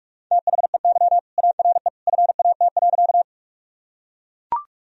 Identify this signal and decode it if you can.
Morse code, heard as a keyed tone, about 43 wpm, 700 Hz. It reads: THEY ARE FAT.
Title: Sentences from Top 1000 Words - Review Part 4 - 45wpm